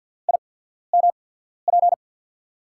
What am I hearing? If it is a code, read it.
Morse code, heard as a keyed tone, about 50 wpm, spaced as for 14 wpm, 700 Hz. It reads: IMP
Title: USA Call Signs - 6 Characters - 8x Character Spacing - 50wpm